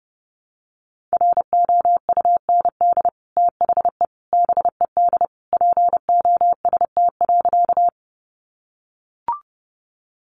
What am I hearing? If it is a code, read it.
Morse code, heard as a keyed tone, about 30 wpm, 700 Hz. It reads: ROUND THE BED POST.
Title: Sentences from Top 1000 Words 30wpm